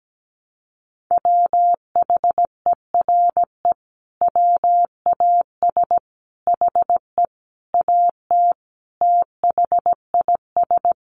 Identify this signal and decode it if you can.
Morse code, heard as a keyed tone, about 17 wpm, 700 Hz. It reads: WHERE WAS HE AT THIS
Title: Sentences from Top 1000 Words 17wpm